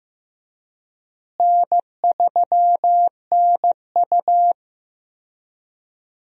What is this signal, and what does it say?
Morse code, heard as a keyed tone, about 16 wpm, 700 Hz. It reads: N3NU
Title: USA Call Signs - 4 Characters - Repeated 3x - 15wpm